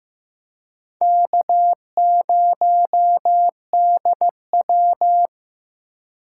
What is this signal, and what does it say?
Morse code, heard as a keyed tone, about 17 wpm, 700 Hz. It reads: K0DW
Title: Speed Racing - USA 1x2 Call Signs 15wpm